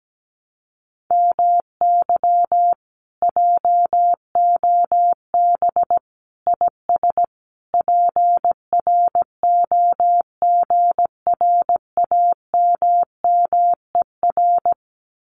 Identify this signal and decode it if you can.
Morse code, heard as a keyed tone, about 17 wpm, 700 Hz. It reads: MY JOB IS PROGRAMMER
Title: Common QSO Phrases Encore 17wpm